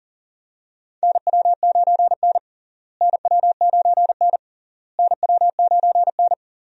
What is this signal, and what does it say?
Morse code, heard as a keyed tone, about 40 wpm, 700 Hz. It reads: NW9N NW9N NW9N
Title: USA Call Signs - 4 Characters - Repeated 3x - 40wpm